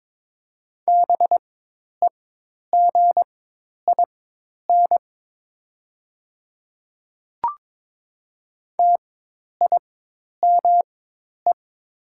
Morse code, heard as a keyed tone, about 22 wpm, spaced as for 10 wpm, 700 Hz. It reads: BEGIN TIME
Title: 4x Character Spacing - Top 500 Words - 22wpm